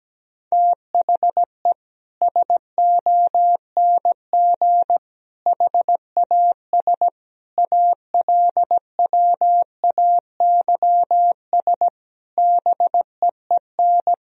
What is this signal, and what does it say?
Morse code, heard as a keyed tone, about 17 wpm, 700 Hz. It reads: THE SONG HAS ALWAYS BEEN
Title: Sentences from Top 1000 Words - Review Part 3 - 17wpm